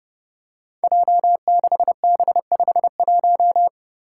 Morse code, heard as a keyed tone, about 32 wpm, 700 Hz. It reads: J6B51
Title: Five Challenging Letter-Number Combinations 30wpm